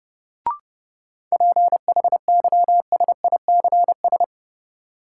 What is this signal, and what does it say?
Morse code, heard as a keyed tone, about 30 wpm, 700 Hz. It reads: PHYSICS
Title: Sentences from Top 1500 Words - New Word Drill Alt - 30wpm